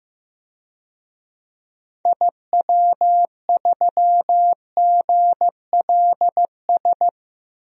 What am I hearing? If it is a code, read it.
Morse code, heard as a keyed tone, about 15 wpm, 700 Hz. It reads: IW3GLS
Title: DXCC-100 Call Signs - 6 Characters - 15wpm